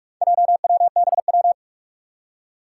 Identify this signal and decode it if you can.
Morse code, heard as a keyed tone, about 46 wpm, 700 Hz. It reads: JWBW
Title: Four Challenging Letter-Number Combinations 45wpm